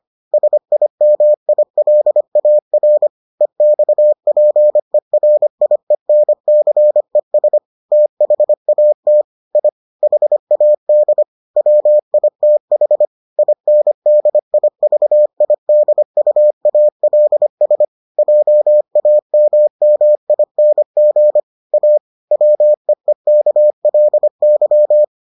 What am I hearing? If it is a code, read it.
Morse code, heard as a keyed tone, about 25 wpm, 600 Hz. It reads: SIMILAR EXPERIENCES THAT I HAD WITH INDIVIDUALS JAMMING A WEEKLY